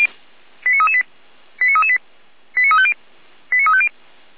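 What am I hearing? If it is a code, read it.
Five-tone selective call (ZVEI), heard as quick runs of five tones, tones of about 70 ms. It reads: 89338 89338 89350 89350
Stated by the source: Five-tone ZVEI selective call received on an SDR, channel or frequency 455.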